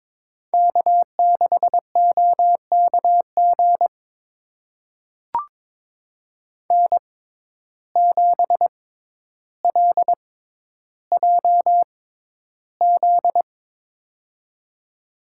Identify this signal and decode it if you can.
Morse code, heard as a keyed tone, about 21 wpm, spaced as for 11 wpm, 700 Hz. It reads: K6OKG N7LJZ
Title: USA Call Signs - 5 Characters - 6x Character Spacing - 22wpm